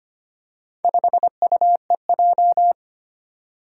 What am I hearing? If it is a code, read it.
Morse code, heard as a keyed tone, about 25 wpm, 700 Hz. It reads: 5UEJ
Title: Four Challenging Letter-Number Combinations 25wpm